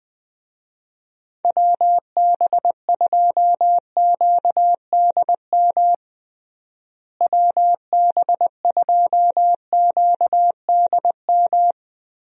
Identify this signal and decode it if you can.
Morse code, heard as a keyed tone, about 21 wpm, 700 Hz. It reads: WB2QDM WB2QDM
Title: USA Call Signs - 6 Characters - Repeated 3x - 20wpm